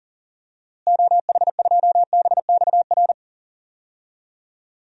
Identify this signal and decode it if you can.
Morse code, heard as a keyed tone, about 40 wpm, 700 Hz. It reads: OH2BXR
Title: DXCC-100 Call Signs - 6 Characters - Repeated 2x - 40wpm